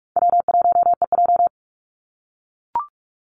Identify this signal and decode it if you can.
Morse code, heard as a keyed tone, about 44 wpm, 700 Hz. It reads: W1EJ
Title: NA SOTA Activator Callsigns 45wpm